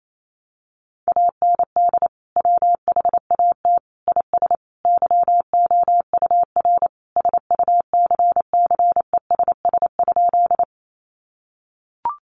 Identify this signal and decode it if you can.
Morse code, heard as a keyed tone, about 28 wpm, 700 Hz. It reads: AND WHAT IS YOUR SUCCESS?
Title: Sentences from Top 900 Words 28wpm